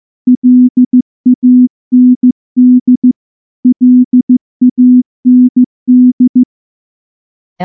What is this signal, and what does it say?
Morse code, heard as a keyed tone, about 14 wpm, 250 Hz. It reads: LAND LAND
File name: morse / morse-training-15WPM